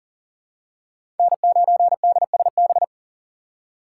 Morse code, heard as a keyed tone, about 40 wpm, 700 Hz. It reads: N9DSB
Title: USA Call Signs - 5 Characters - 4x Character Spacing - 40wpm